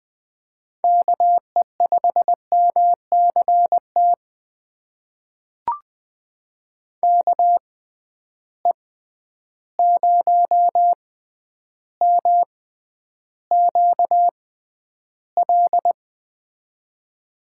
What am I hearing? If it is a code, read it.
Morse code, heard as a keyed tone, about 20 wpm, spaced as for 10 wpm, 700 Hz. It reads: KE5MCT KE0MQL
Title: USA Call Signs - 6 Characters - 6x Character Spacing - 20wpm